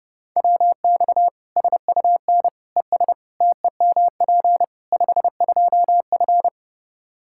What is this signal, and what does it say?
Morse code, heard as a keyed tone, about 30 wpm, 700 Hz. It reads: WX SUN ES TEMP 52F
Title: Practice QSO - Exchange 2 - Info - 30wpm